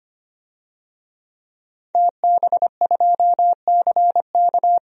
Morse code, heard as a keyed tone, about 25 wpm, 700 Hz. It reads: TB2CK
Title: DXCC-100 Call Signs - 5 Characters - Repeated 2x - 25wpm